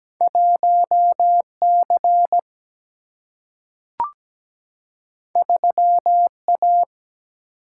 Morse code, heard as a keyed tone, about 17 wpm, 700 Hz. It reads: 1 C 3 A
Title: Single-Letter Single-Number 17wpm